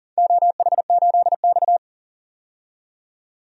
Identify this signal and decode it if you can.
Morse code, heard as a keyed tone, about 42 wpm, 700 Hz. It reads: OH8X
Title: DXCC-100 Call Signs - 4 Characters - 40wpm